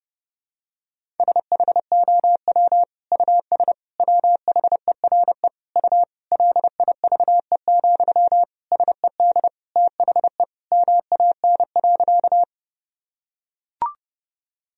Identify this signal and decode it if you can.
Morse code, heard as a keyed tone, about 32 wpm, 700 Hz. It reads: SHOW US WHERE U LIVE, SED THE MAN.
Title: Sentences from Top 200 Words and Abbreviations 30wpm